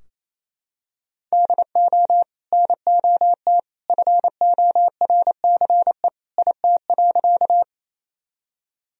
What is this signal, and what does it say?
Morse code, heard as a keyed tone, about 28 wpm, 700 Hz. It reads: DO NOT FORCE IT.
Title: Sentences from Top 1000 Words - Review Part 3 - 28wpm